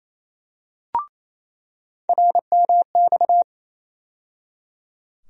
Morse code, heard as a keyed tone, about 27 wpm, 700 Hz. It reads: RMX
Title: USA Call Sign Postfixes - 3 Letters 28wpm